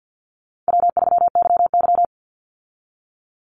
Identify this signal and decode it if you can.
Morse code, heard as a keyed tone, about 50 wpm, 700 Hz. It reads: W3YY
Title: Call Signs Repeated 2x - 50wpm